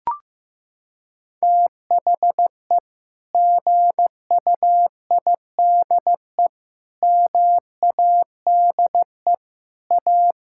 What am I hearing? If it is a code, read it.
Morse code, heard as a keyed tone, about 15 wpm, 700 Hz. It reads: THE GUIDE MADE A
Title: Sentences from Top 1000 Words - Review Part 6 - 15wpm